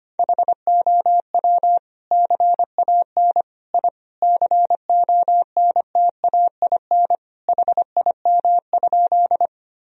Morse code, heard as a keyed tone, about 25 wpm, 700 Hz. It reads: HOW CAN I CONTAIN HIM?
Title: Sentences from Top 1000 Words - Review Part 2 - 25wpm